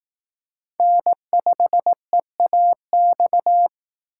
Morse code, heard as a keyed tone, about 18 wpm, 700 Hz. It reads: N5EAX
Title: Speed Racing - USA 1x3 Call Signs 15wpm